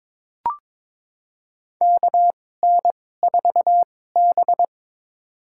Morse code, heard as a keyed tone, about 22 wpm, spaced as for 16 wpm, 700 Hz. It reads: KN4B
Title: USA Call Signs - 4 Characters - 2x Character Spacing - 22wpm